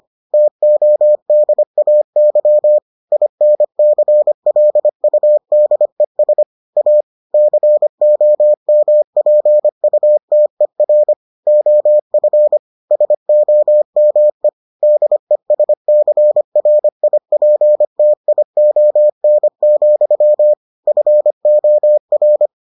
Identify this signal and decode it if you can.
Morse code, heard as a keyed tone, about 25 wpm, 600 Hz. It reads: TODAY INCLUDES A COMPUTER OF SOME DESCRIPTION, FOR